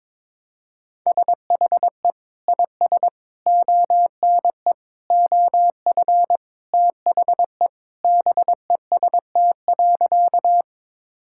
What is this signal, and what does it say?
Morse code, heard as a keyed tone, about 22 wpm, 700 Hz. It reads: SHE IS ONE OF THE BEST.
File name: practice / 22wpm / Sentences with 2-Word Phrases - 22wpm